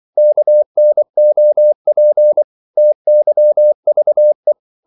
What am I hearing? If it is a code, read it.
Morse code, heard as a keyed tone, about 24 wpm, 600 Hz. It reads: KNOP TYVE
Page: Korte danske ord 24wpm